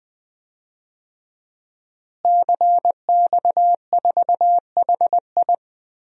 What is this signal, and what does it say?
Morse code, heard as a keyed tone, about 20 wpm, 700 Hz. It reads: CX4HI
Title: DXCC-100 Call Signs - 5 Characters - Repeated 2x - 20wpm